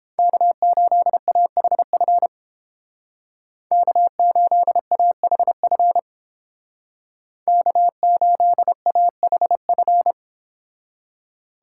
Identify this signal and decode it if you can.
Morse code, heard as a keyed tone, about 31 wpm, 700 Hz. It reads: K8AHF K8AHF K8AHF